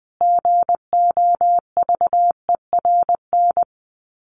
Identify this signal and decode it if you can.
Morse code, heard as a keyed tone, about 20 wpm, 700 Hz. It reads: GOVERN
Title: Top 500 Words Repeated 20wpm